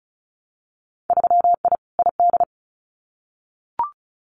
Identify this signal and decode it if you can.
Morse code, heard as a keyed tone, about 34 wpm, 700 Hz. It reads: 3I ID
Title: Winter Field Day Exchanges 35wpm